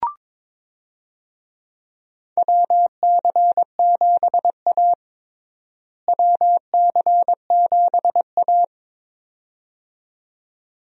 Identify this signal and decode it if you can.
Morse code, heard as a keyed tone, about 22 wpm, 700 Hz. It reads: WC7A WC7A
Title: USA Call Signs - 4 Characters - Repeated 2x - 22wpm